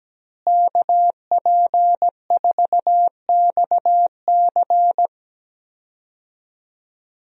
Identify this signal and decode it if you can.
Morse code, heard as a keyed tone, about 17 wpm, 700 Hz. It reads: KP4XC
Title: DXCC-100 Call Signs - 5 Characters - Repeated 3x - 17wpm